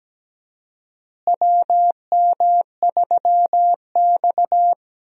Morse code, heard as a keyed tone, about 17 wpm, 700 Hz. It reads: WM3X
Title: USA Call Signs - 4 Characters - Repeated 3x - 17wpm